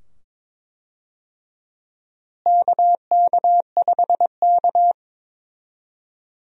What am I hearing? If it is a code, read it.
Morse code, heard as a keyed tone, about 22 wpm, 700 Hz. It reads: KK5K
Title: USA Call Signs - 4 Characters - Repeated 2x - 22wpm